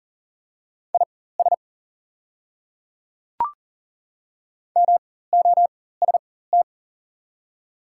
Morse code, heard as a keyed tone, about 40 wpm, spaced as for 19 wpm, 690 Hz. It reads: IS MOST
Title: 4x Character Spacing - Top 100 Words - 40wpm